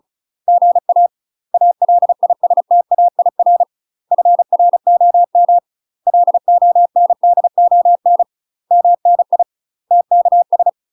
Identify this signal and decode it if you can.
Morse code, heard as a keyed tone, about 34 wpm, 700 Hz. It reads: GA ALISTAIR FROM LONDON MNI TKS
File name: UK MASTER QSOs TRAINING / 35WPM / UK MASTER QSOs 0039